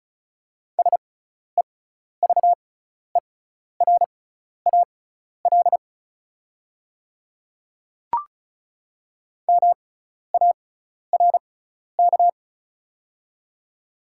Morse code, heard as a keyed tone, about 34 wpm, spaced as for 12 wpm, 690 Hz. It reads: SEVERAL MARK
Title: 6x Character Spacing - Top 500 Words - 35wpm